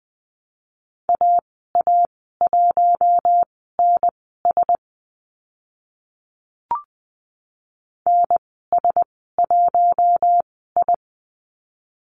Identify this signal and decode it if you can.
Morse code, heard as a keyed tone, about 20 wpm, spaced as for 15 wpm, 700 Hz. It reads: AA1NS NS1I
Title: 042 Lesson 8g - Number 1 - Call Signs 20wpm